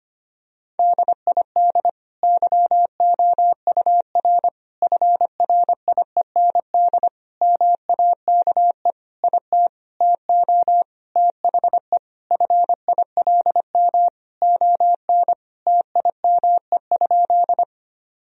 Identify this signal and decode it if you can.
Morse code, heard as a keyed tone, about 25 wpm, 700 Hz. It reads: DID YOUR FRIEND MAKE IT TO THE FILM ON TIME?
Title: Sentences from Top 1000 Words - Review Part 3 - 25wpm